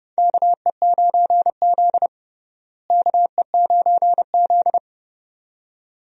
Morse code, heard as a keyed tone, about 30 wpm, 700 Hz. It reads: KE9Z KE9Z
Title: USA Call Signs - 4 Characters - Repeated 2x - 30wpm